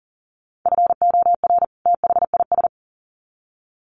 Morse code, heard as a keyed tone, about 40 wpm, 700 Hz. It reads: FOR THIS
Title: Sets of 2 Words 40wpm